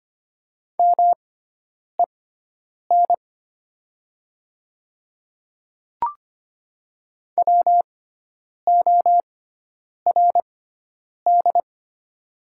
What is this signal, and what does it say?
Morse code, heard as a keyed tone, about 25 wpm, spaced as for 9 wpm, 700 Hz. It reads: MEN WORD
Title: 6x Character Spacing - Top 200 Words - 25wpm